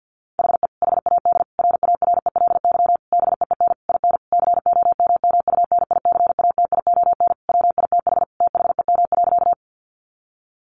Morse code, heard as a keyed tone, about 50 wpm, 700 Hz. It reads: HE HAD RARELY BEEN IN COMMUNICATION WITH THEM.
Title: Sentences from Top 1400 Words 50wpm